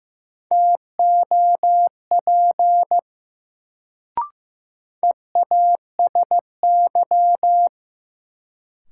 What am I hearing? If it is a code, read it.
Morse code, heard as a keyed tone, about 15 wpm, 700 Hz. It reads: TOP EASY
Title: Sentences from Top 700 Words - New Word Drill - 15wpm